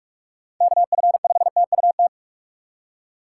Morse code, heard as a keyed tone, about 46 wpm, 700 Hz. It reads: KF5TUT
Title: USA Call Signs - 6 Characters - 2x Character Spacing - 45wpm